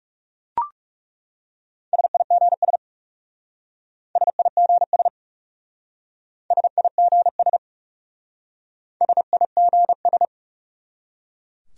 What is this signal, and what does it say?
Morse code, heard as a keyed tone, about 38 wpm, 700 Hz. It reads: SIGS SIGS SIGS SIGS